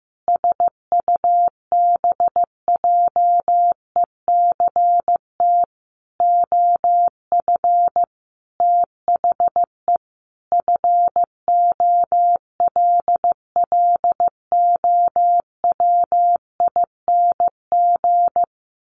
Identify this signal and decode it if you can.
Morse code, heard as a keyed tone, about 15 wpm, 700 Hz. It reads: SUBJECT OF THE FOLLOWING
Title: Sentences from Top 900 Words 15wpm